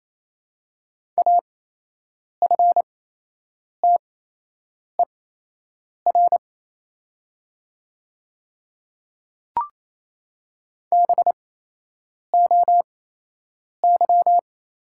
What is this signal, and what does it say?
Morse code, heard as a keyed tone, about 27 wpm, spaced as for 8 wpm, 700 Hz. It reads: AFTER BOY
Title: 8x Character Spacing - Top 200 Words - 28wpm